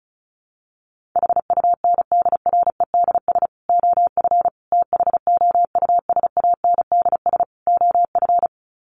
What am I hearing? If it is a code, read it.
Morse code, heard as a keyed tone, about 34 wpm, 700 Hz. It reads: HUNDREDS OF THOUSANDS OF
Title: Sets of 4 Words 35wpm